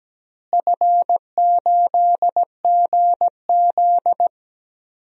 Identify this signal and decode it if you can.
Morse code, heard as a keyed tone, about 17 wpm, 700 Hz. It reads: F8GZ
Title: DXCC-100 Call Signs - 4 Characters - Repeated 2x - 17wpm